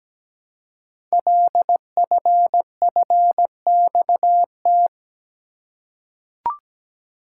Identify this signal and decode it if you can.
Morse code, heard as a keyed tone, about 17 wpm, 700 Hz. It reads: LFFXT
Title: Code Groups - 5 Letters - 17wpm